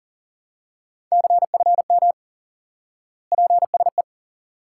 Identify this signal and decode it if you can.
Morse code, heard as a keyed tone, about 40 wpm, 700 Hz. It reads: CFM PSE
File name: IQE Course 3 - Lesson 89 - YAGI - Two Elements - 40wpm